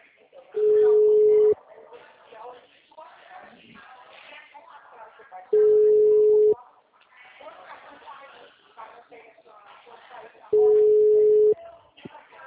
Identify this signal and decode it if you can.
Ringback tone (Europe, 425 Hz), with speech.
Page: Atibaia - SP 1